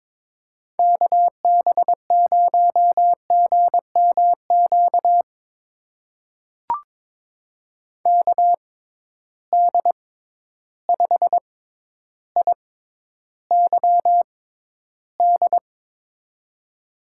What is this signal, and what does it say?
Morse code, heard as a keyed tone, about 22 wpm, spaced as for 11 wpm, 700 Hz. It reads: KB0GMQ KD5IYD
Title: USA Call Signs - 6 Characters - 6x Character Spacing - 22wpm